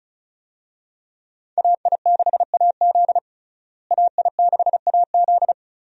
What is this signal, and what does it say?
Morse code, heard as a keyed tone, about 35 wpm, 700 Hz. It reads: AI6AZ AI6AZ
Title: USA Call Signs - 5 Characters - Repeated 2x - 35wpm